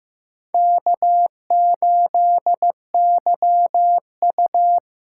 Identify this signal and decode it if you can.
Morse code, heard as a keyed tone, about 15 wpm, 700 Hz. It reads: K8YU
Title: USA Call Signs - 4 Characters - 4x Character Spacing - 15wpm